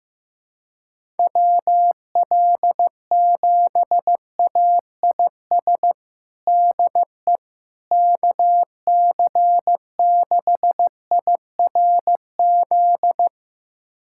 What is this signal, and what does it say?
Morse code, heard as a keyed tone, about 15 wpm, 700 Hz. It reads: WL7AIS DE KC6IRZ
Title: USA Callsign DE Callsign 15wpm